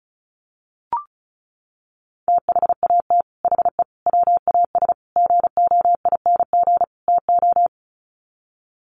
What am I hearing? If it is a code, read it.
Morse code, heard as a keyed tone, about 34 wpm, 700 Hz. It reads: THAT HE WAS GOING TO
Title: Sets of 5 Words 35wpm